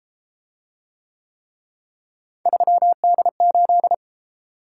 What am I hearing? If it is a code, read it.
Morse code, heard as a keyed tone, about 33 wpm, 700 Hz. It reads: 3D8